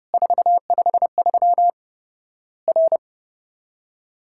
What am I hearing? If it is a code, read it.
Morse code, heard as a keyed tone, about 28 wpm, 700 Hz. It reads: 453 R
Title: Worked All Europe DX Contest - QTC Exchange - 30wpm